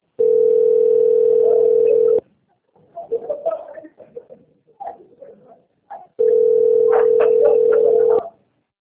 Ringback tone (North America), 440 + 480 Hz.
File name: out- - - - - .1086